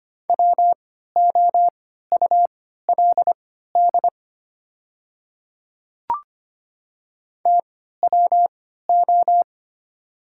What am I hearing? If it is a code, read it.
Morse code, heard as a keyed tone, about 25 wpm, spaced as for 14 wpm, 700 Hz. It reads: WOULD TWO
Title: 3x Character Spacing - Top 100 Words - 25wpm